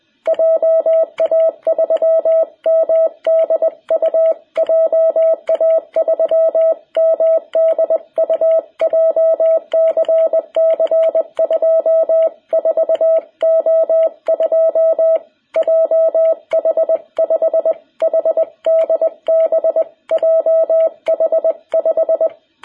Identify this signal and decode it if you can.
Morse code, heard as a keyed tone, about 20 wpm, 660 Hz. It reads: JA3MDU JA3MDU JCC 24O2 JH5HDB JH5
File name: jh8cbh / 2021cw